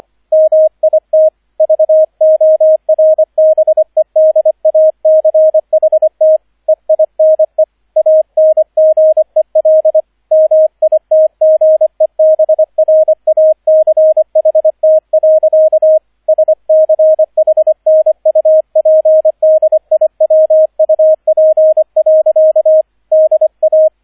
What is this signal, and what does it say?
Morse code, heard as a keyed tone, about 25 wpm, 630 Hz, with radio noise. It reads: MIT VORBEDACHT EINE ANGEL MITGEBRACHT. SCHNUPDIWUP. DA